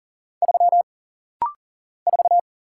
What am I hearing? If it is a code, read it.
Morse code, heard as a keyed tone, about 40 wpm, 700 Hz. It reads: 3 4
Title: Single Letter-Number - Rapid-Fire 40wpm